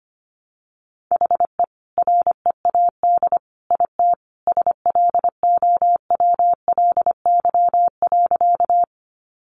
Morse code, heard as a keyed tone, about 25 wpm, 700 Hz. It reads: HE READ IT SLOWLY.
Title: Sentences from Top 1100 Words 25wpm